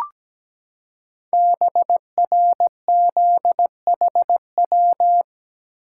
Morse code, heard as a keyed tone, about 17 wpm, 700 Hz. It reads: BRZHW
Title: Code Groups - 5 Letters - 17wpm